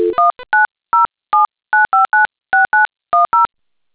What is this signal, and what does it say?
DTMF, heard as telephone touch tones, tones of about 150 ms. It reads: 19079596910